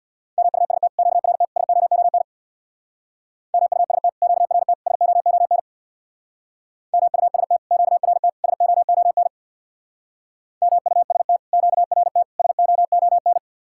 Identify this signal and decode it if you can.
Morse code, heard as a keyed tone, about 60 wpm, 700 Hz. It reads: MUST QRT SOON MUST QRT SOON MUST QRT SOON MUST QRT SOON
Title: Speed-Racing Common QSO Phrases Encore - 50wpm